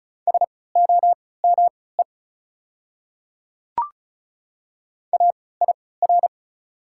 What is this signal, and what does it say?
Morse code, heard as a keyed tone, about 36 wpm, spaced as for 20 wpm, 700 Hz. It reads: SOME AIR
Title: 3x Character Spacing - Top 200 Words - 35wpm